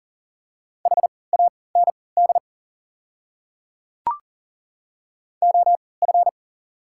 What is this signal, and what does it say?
Morse code, heard as a keyed tone, about 38 wpm, spaced as for 22 wpm, 700 Hz. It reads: HAND OF
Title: 3x Character Spacing - Top 200 Words - 40wpm